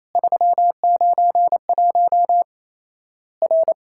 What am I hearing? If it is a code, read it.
Morse code, heard as a keyed tone, about 28 wpm, 700 Hz. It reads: 391 R
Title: Worked All Europe DX Contest - QTC Exchange - 28wpm